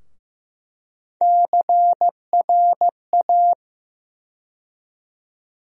Morse code, heard as a keyed tone, about 15 wpm, 700 Hz. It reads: CRA
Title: USA Call Sign Postfixes - 3 Letters 15wpm